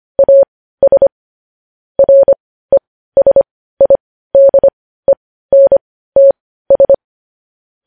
Morse code, heard as a keyed tone, about 25 wpm, spaced as for 15 wpm, 550 Hz. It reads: AS RESIDENTS
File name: podcast / News headlines-1x-25x15